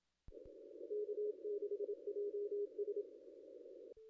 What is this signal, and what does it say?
Morse code, heard as a keyed tone, about 26 wpm, 400 Hz. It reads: K6JS
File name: contests / ss-cw-k5tr-op-n5rz -0600